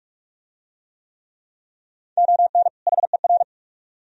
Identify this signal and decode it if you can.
Morse code, heard as a keyed tone, about 39 wpm, 700 Hz. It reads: ON HER